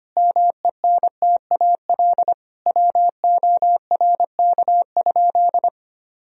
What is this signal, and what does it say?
Morse code, heard as a keyed tone, about 25 wpm, 700 Hz. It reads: MENTAL WORK?
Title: Sentences from Top 1000 Words - Review Part 4 - 25wpm